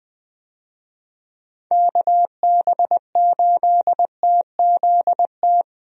Morse code, heard as a keyed tone, about 20 wpm, 700 Hz. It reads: KB8TZT
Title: USA Call Signs - 6 Characters - 20wpm